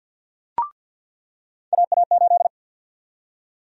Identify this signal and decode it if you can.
Morse code, heard as a keyed tone, about 47 wpm, 700 Hz. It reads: AA8
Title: USA Call Sign Prefixes - 50wpm